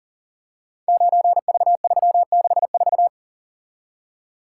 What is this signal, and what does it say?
Morse code, heard as a keyed tone, about 40 wpm, 700 Hz. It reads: 9V364
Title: Code Groups - 5 Letter:Numbers - 40wpm